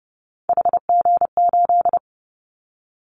Morse code, heard as a keyed tone, about 30 wpm, 700 Hz. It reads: HG8
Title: Callsign Prefixes for Top 25 DXCC - Repeated 3x - 30wpm